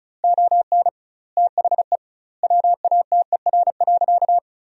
Morse code, heard as a keyed tone, about 34 wpm, 700 Hz. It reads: ON THE WATER.
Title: 2x Word-Spacing - Sentences from Top 500 Words - 35wpm